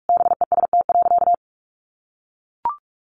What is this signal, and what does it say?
Morse code, heard as a keyed tone, about 44 wpm, 700 Hz. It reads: BEST.
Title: Sentences from Top 500 Words 45wpm